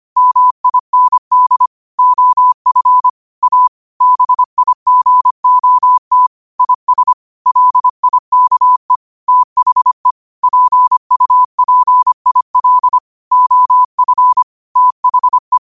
Morse code, heard as a keyed tone, about 25 wpm, 1000 Hz. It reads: MIND OF A BIGOT IS LIKE THE PUPIL OF THE
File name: qotdfiles / 25wpm